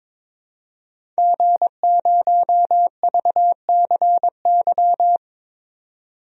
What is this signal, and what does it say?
Morse code, heard as a keyed tone, about 22 wpm, 700 Hz. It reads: G0VCY
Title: DXCC-100 Call Signs - 5 Characters - 22wpm